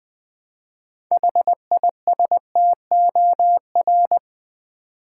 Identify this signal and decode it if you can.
Morse code, heard as a keyed tone, about 20 wpm, 700 Hz. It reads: HISTOR
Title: Build A Word - 3 Letters At A Time - 9-Letter Words - 20wpm